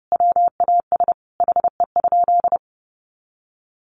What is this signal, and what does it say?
Morse code, heard as a keyed tone, about 30 wpm, 700 Hz. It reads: WAS HE?
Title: Sentences from Top 1300 Words 30wpm